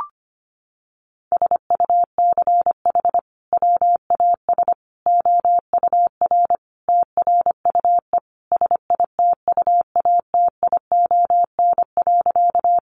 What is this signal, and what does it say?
Morse code, heard as a keyed tone, about 25 wpm, 700 Hz. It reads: SUCH WAS OUR TRUE SITUATION.
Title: Sentences from Top 1200 Words 25wpm